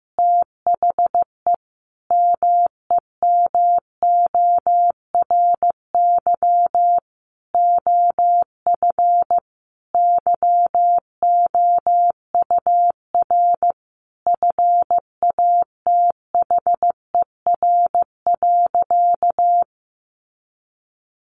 Morse code, heard as a keyed tone, about 15 wpm, 700 Hz. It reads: THE MEMORY OF YOUR FATHER.
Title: Sentences from Top 1300 Words 15wpm